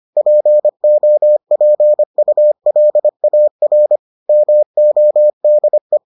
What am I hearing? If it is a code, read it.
Morse code, heard as a keyed tone, about 25 wpm, 600 Hz. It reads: POPULAR MODE